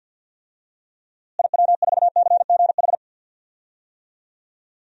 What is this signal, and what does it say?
Morse code, heard as a keyed tone, about 46 wpm, 700 Hz. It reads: IW4CGH
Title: DXCC-100 Call Signs - 6 Characters - Repeated 3x - 50wpm